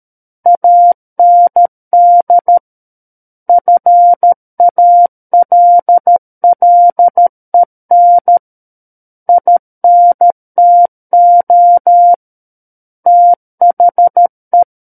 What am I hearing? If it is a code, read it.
Morse code, heard as a keyed tone, about 13 wpm, 700 Hz. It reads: AND FALLEN INTO THE
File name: Book One / 04-Chapter 4 - 13 WPM